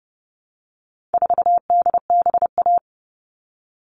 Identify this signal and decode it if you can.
Morse code, heard as a keyed tone, about 30 wpm, 700 Hz. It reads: 4DBA
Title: All Cut Numbers - 4-Digit 30wpm